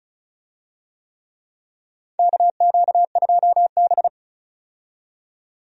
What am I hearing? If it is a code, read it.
Morse code, heard as a keyed tone, about 36 wpm, 700 Hz. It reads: KQ2B
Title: USA Call Signs - 4 Characters - 35wpm